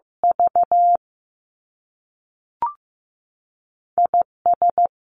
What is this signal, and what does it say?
Morse code, heard as a keyed tone, about 15 wpm, 700 Hz. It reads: V I S
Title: Two Challenging Letter-Number Combinations 15wpm